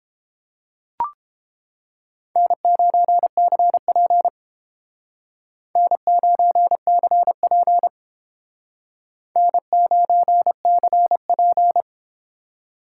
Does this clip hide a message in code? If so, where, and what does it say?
Yes: Morse code from 2.4 to 11.8 s, heard as a keyed tone, about 30 wpm, 700 Hz. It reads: N9CP N9CP N9CP